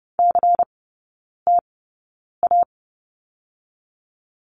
Morse code, heard as a keyed tone, about 30 wpm, spaced as for 9 wpm, 700 Hz. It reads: CTA
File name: Lesson 17e - Letter u - Three Characters 30wpm